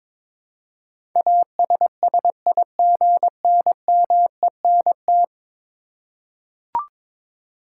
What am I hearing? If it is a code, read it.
Morse code, heard as a keyed tone, about 22 wpm, 700 Hz. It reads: ASSIGNMENT
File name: practice / 22wpm / 10 Letter Words - First 3 Letters Given - 22wpm